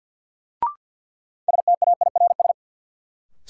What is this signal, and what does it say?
Morse code, heard as a keyed tone, about 53 wpm, 700 Hz. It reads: STAIRS
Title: Sets of 5 Words Encore - New Word Drill - 50wpm